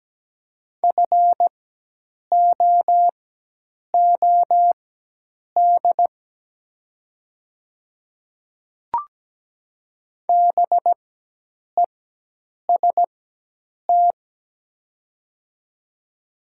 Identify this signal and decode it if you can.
Morse code, heard as a keyed tone, about 17 wpm, spaced as for 8 wpm, 700 Hz. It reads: FOOD BEST
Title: 4x Character Spacing - Top 500 Words - 17wpm